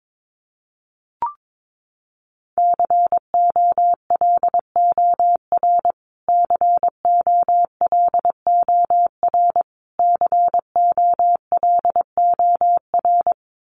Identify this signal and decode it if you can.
Morse code, heard as a keyed tone, about 22 wpm, 700 Hz. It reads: COLOR COLOR COLOR
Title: Top 500 Words Repeated 22wpm